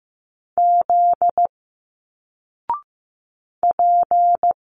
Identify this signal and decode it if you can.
Morse code, heard as a keyed tone, about 15 wpm, 700 Hz. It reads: Z P
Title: Single Letters 15wpm